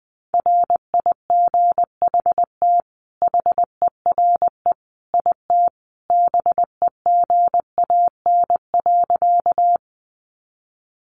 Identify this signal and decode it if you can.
Morse code, heard as a keyed tone, about 20 wpm, 700 Hz. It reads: RIGHT HERE IT BEGAN.
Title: Sentences from Top 300 Words 20wpm